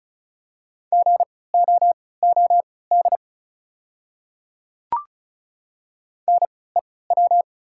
Morse code, heard as a keyed tone, about 36 wpm, spaced as for 20 wpm, 690 Hz. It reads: GOOD NEW
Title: 3x Character Spacing - Top 200 Words - 35wpm